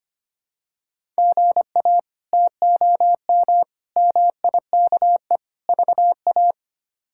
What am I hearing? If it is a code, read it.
Morse code, heard as a keyed tone, about 25 wpm, 700 Hz. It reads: GA TOM MIKE VA
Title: K1USN SST QSO Practice - 25wpm